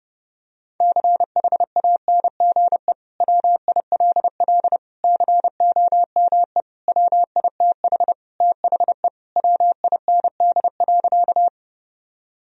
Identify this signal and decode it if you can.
Morse code, heard as a keyed tone, about 30 wpm, 700 Hz. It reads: CHANGE WILL COME WITH THE WIND.
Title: Sentences from Top 1000 Words - Review Part 2 - 30wpm